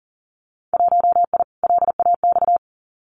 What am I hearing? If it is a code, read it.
Morse code, heard as a keyed tone, about 41 wpm, 700 Hz. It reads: 1I LAX
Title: Winter Field Day Exchanges 40wpm